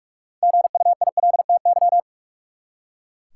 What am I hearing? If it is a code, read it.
Morse code, heard as a keyed tone, about 42 wpm, 700 Hz. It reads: GUILTY
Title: Six Letter Words - 45wpm